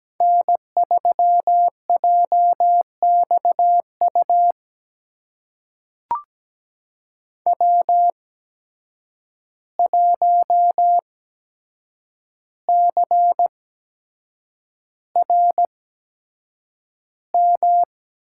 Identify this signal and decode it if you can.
Morse code, heard as a keyed tone, about 17 wpm, spaced as for 7 wpm, 700 Hz. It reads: N3JXU W1CRM
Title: USA Call Signs - 5 Characters - 8x Character Spacing - 17wpm